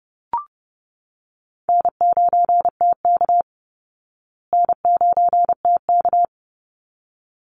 Morse code, heard as a keyed tone, about 30 wpm, 700 Hz. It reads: N9TK N9TK
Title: Easy Call Signs Repeated 2x - 30wpm